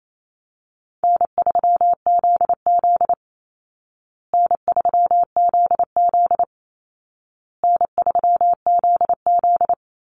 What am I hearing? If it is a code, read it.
Morse code, heard as a keyed tone, about 28 wpm, 700 Hz. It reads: N3ZZ N3ZZ N3ZZ
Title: Easy Call Signs Repeated 3x - 28wpm